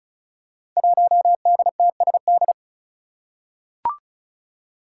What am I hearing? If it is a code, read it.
Morse code, heard as a keyed tone, about 36 wpm, 700 Hz. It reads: 1DTSD
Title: Five Challenging Letter-Number Combinations 35wpm